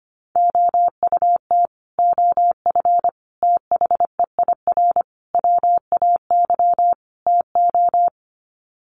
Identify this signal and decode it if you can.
Morse code, heard as a keyed tone, about 25 wpm, 700 Hz. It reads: OUT OF THEIR WAY TO
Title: Sets of 5 Words Encore2 25wpm